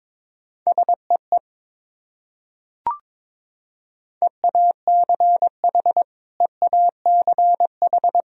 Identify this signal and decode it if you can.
Morse code, heard as a keyed tone, about 22 wpm, 700 Hz. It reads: SEE EACH EACH
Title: Top 100 Words Repeated 22wpm